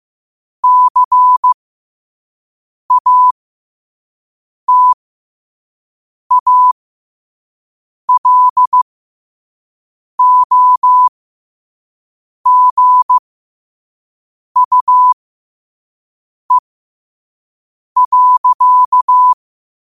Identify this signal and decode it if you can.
Morse code, heard as a keyed tone, about 15 wpm, spaced as for 5 wpm, 1000 Hz. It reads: CATALOGUE.
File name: qotdfiles / 5wpm